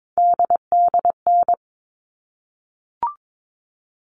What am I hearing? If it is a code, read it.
Morse code, heard as a keyed tone, about 22 wpm, 700 Hz. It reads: DDN
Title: All Cut Numbers - 3-Digit 22wpm